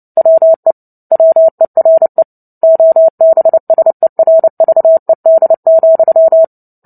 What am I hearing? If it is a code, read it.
Morse code, heard as a keyed tone, about 30 wpm, 650 Hz. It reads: WE WERE OBSERVED,
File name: Book Two / 03-Chapter 3 - 29 WPM